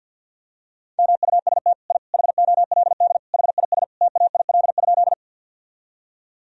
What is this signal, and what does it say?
Morse code, heard as a keyed tone, about 47 wpm, 700 Hz. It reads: MUST I HOLD HIS TAIL?
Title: Sentences from Top 1000 Words - Review Part 3 - 50wpm